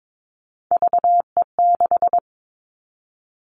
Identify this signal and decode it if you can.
Morse code, heard as a keyed tone, about 22 wpm, 700 Hz. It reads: VE6
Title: Callsign Prefixes for Top 50 DXCC - 22wpm